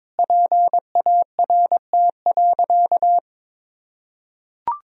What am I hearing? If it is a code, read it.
Morse code, heard as a keyed tone, about 21 wpm, 700 Hz. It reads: PART.
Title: Sentences from Top 1000 Words - Review Part 1 - 22wpm